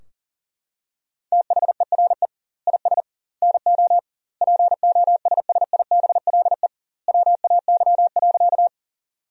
Morse code, heard as a keyed tone, about 40 wpm, 700 Hz. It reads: THERE IS NO POSSIBLE WAY.
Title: 2x Word-Spacing - Sentences from Top 500 Words - 40wpm